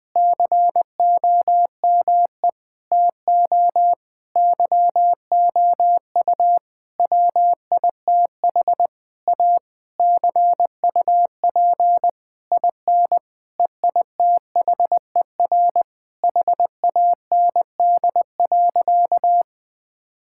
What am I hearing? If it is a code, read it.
Morse code, heard as a keyed tone, about 20 wpm, 700 Hz. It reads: COME TO YOU WITH A CUP IN EITHER HAND.
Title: Sentences from Top 1600 Words 20wpm